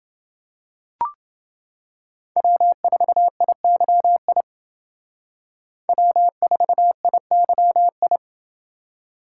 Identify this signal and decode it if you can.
Morse code, heard as a keyed tone, about 29 wpm, 700 Hz. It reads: W4IYI W4IYI